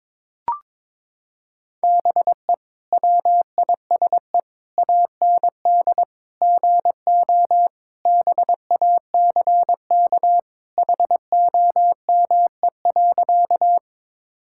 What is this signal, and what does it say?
Morse code, heard as a keyed tone, about 22 wpm, 700 Hz. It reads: BE WISE AND GO BACK HOME.
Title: Sentences from Top 1600 Words 22wpm